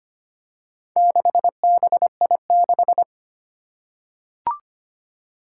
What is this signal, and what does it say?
Morse code, heard as a keyed tone, about 25 wpm, 700 Hz. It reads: 6BI6
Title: Four Challenging Letter-Number Combinations 25wpm